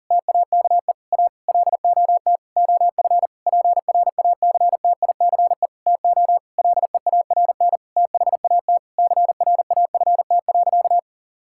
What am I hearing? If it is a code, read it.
Morse code, heard as a keyed tone, about 37 wpm, 700 Hz. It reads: TAKE A LOT OF PRACTICE TO LEARN THAT CRAFT.
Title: Sentences from Top 1000 Words - Review Part 6 - 40wpm